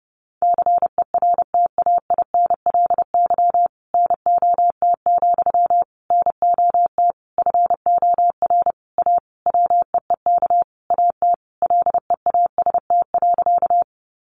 Morse code, heard as a keyed tone, about 28 wpm, 700 Hz. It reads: CERTAINLY NOT, NOT FOR A WEEK AT LEAST.
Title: Sentences from Top 900 Words 30wpm